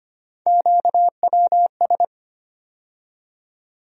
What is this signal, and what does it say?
Morse code, heard as a keyed tone, about 25 wpm, 700 Hz. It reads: QWS
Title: USA Call Sign Postfixes - 3 Letters 25wpm